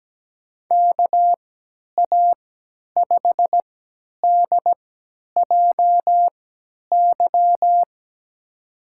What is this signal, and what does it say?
Morse code, heard as a keyed tone, about 17 wpm, spaced as for 10 wpm, 700 Hz. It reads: KA5DJY
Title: USA Call Signs - 3x Character Spacing - 17wpm